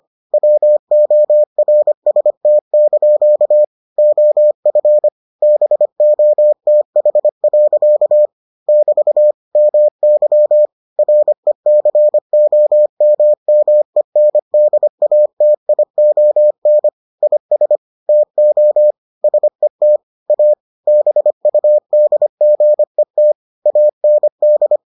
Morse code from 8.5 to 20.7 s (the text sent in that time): <BT> MY RECOMMENDATION IS TO SET A